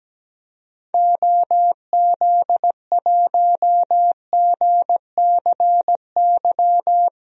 Morse code, heard as a keyed tone, about 17 wpm, 700 Hz. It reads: OZ1GCY
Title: DXCC-100 Call Signs - 6 Characters - Repeated 3x - 17wpm